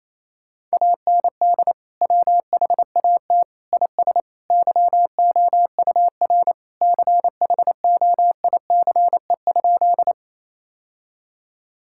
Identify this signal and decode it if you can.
Morse code, heard as a keyed tone, about 27 wpm, 700 Hz. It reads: AND WHAT IS YOUR CHOICE?
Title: Sentences from Top 1000 Words - Review Part 6 - 28wpm